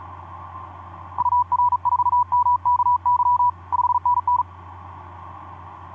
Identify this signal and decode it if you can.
Morse code, heard as a keyed tone, about 37 wpm, 980 Hz. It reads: UR4MKY 5NN
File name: d44ac ek2rgt etc 10m cw 23nov2010 1202